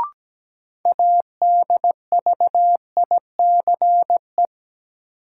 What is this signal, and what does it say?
Morse code, heard as a keyed tone, about 17 wpm, 700 Hz. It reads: ADVICE
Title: Sentences from Top 1100 Words - New Word Drill - 17wpm